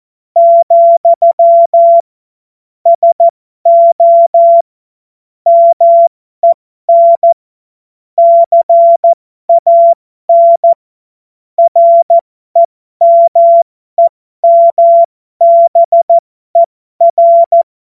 Morse code, heard as a keyed tone, about 14 wpm, spaced as for 12 wpm, 680 Hz. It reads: , SO MEN CAN REMEMBER